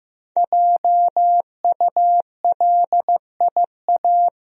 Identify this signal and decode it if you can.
Morse code, heard as a keyed tone, about 15 wpm, 700 Hz. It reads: JULIA
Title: USA Common Names - 15wpm